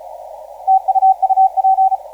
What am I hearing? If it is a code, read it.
Morse code, heard as a keyed tone, about 34 wpm, 760 Hz, with radio noise. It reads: TU UP
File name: VK9CZ 0910UTC 15m